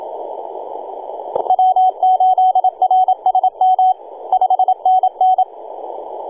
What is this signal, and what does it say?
Morse code, heard as a keyed tone, about 27 wpm, 750 Hz, with radio noise. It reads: W8RSM 5NN
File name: e6gg-18-083-cw